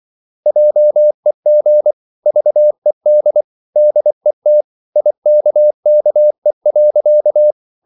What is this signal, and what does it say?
Morse code, heard as a keyed tone, about 24 wpm, 600 Hz. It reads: JEG VED DET IKKE.
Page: Ferie 24 wpm